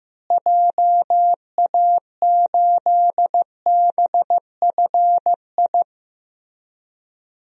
Morse code, heard as a keyed tone, about 15 wpm, 700 Hz. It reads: JA8BFI
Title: DXCC-100 Call Signs - 6 Characters - Repeated 3x - 15wpm